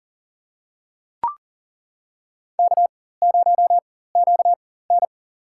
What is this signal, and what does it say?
Morse code, heard as a keyed tone, about 40 wpm, spaced as for 19 wpm, 700 Hz. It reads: K0QN
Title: USA Call Signs - 4 Characters - 4x Character Spacing - 40wpm